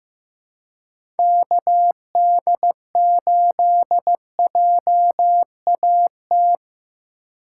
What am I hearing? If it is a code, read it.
Morse code, heard as a keyed tone, about 16 wpm, 700 Hz. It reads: KD8JAT
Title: USA Call Signs - 6 Characters - 3x Character Spacing - 15wpm